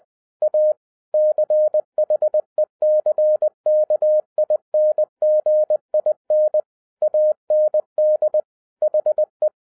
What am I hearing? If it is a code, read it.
Morse code, heard as a keyed tone, about 20 wpm, 610 Hz. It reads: A CHECKINGIN AND HE
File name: 2012-01-mp3 / n16-THROUGH EIGHT MONTHS-20wpm